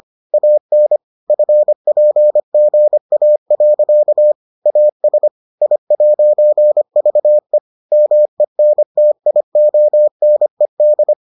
Morse code, heard as a keyed tone, about 25 wpm, 600 Hz. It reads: AN FPGA. AS I'VE MENTIONED